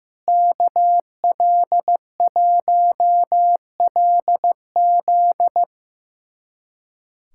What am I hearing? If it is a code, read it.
Morse code, heard as a keyed tone, about 16 wpm, 700 Hz. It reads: KL1LZ
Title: USA Callsign DE Callsign - 5 Characters - 15wpm